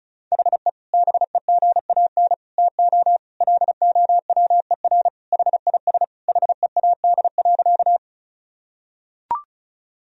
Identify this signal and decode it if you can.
Morse code, heard as a keyed tone, about 36 wpm, 700 Hz. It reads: HE BEGAN TO LOWER HIS HEAD.
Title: Sentences from Top 1000 Words - Review Part 4 - 35wpm